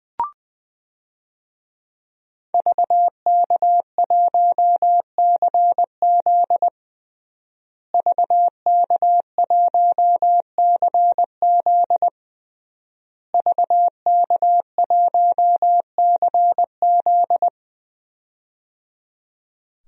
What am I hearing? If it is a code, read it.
Morse code, heard as a keyed tone, about 20 wpm, 700 Hz. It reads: VK1CZ VK1CZ VK1CZ
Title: DXCC-100 Call Signs - 5 Characters - Repeated 3x - 20wpm